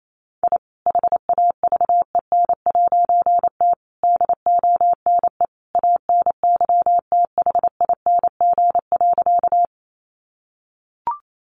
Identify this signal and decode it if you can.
Morse code, heard as a keyed tone, about 28 wpm, 700 Hz. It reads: I HAVEN'T DONE ANYTHING.
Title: A Sound Of Thunder 28wpm